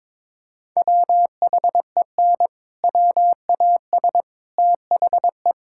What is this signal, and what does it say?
Morse code, heard as a keyed tone, about 22 wpm, 700 Hz. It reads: WHEN WAS THE
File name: practice / 22wpm / Sets of 8 Words - 22wpm